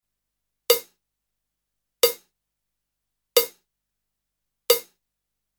There is no code.